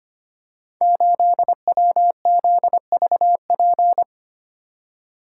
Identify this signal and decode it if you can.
Morse code, heard as a keyed tone, about 25 wpm, 700 Hz. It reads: 8WZVP
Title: Code Groups - 5 Letter:Numbers - 25wpm